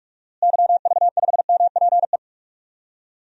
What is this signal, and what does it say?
Morse code, heard as a keyed tone, about 41 wpm, 700 Hz. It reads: YV5MPE
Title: DXCC-100 Call Signs - 6 Characters - Repeated 3x - 45wpm